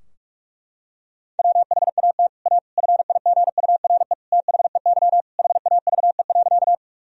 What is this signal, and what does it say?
Morse code, heard as a keyed tone, about 46 wpm, 700 Hz. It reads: WHAT A FIGURE THEY HAVE.
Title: Sentences from Top 1000 Words - Review Part 2 - 45wpm